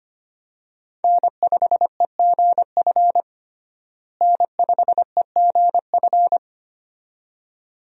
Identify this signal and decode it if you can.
Morse code, heard as a keyed tone, about 25 wpm, 700 Hz. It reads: N5EGF N5EGF
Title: USA Call Signs - 5 Characters - Repeated 2x - 25wpm